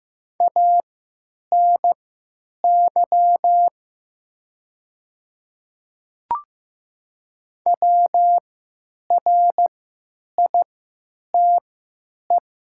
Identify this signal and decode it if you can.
Morse code, heard as a keyed tone, about 15 wpm, spaced as for 9 wpm, 700 Hz. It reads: ANY WRITE
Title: 3x Character Spacing - Top 200 Words - 15wpm